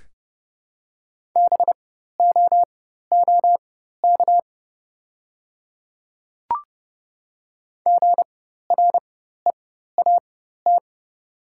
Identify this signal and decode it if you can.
Morse code, heard as a keyed tone, about 30 wpm, spaced as for 14 wpm, 700 Hz. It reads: BOOK GREAT
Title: 4x Character Spacing - Top 500 Words - 30wpm